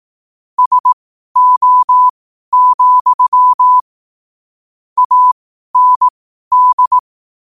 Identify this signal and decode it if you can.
Morse code, heard as a keyed tone, about 18 wpm, spaced as for 12 wpm, 1000 Hz. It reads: SO, AND
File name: qotdfiles / 12wpm